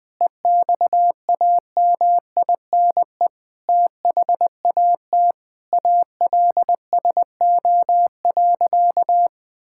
Morse code, heard as a keyed tone, about 21 wpm, 700 Hz. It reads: EXAMINE THAT ALSO.
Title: Sentences from Top 1600 Words 20wpm